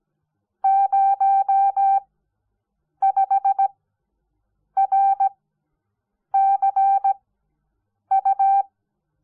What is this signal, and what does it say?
Morse code, heard as a keyed tone, about 17 wpm, spaced as for 7 wpm, 780 Hz. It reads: 05RCU
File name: Onlinelehrgang / morsen / DJ4UF Morsen05